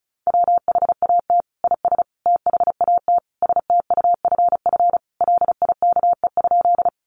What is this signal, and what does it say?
Morse code, heard as a keyed tone, about 34 wpm, 700 Hz. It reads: WHAT IS THAT STUFF LIKE?
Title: Sentences from Top 1400 Words 35wpm